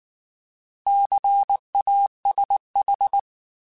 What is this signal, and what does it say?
Morse code, heard as a keyed tone, about 19 wpm, 800 Hz. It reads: CASH
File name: Alphabetical / Alphabetical mp3s / Cs @18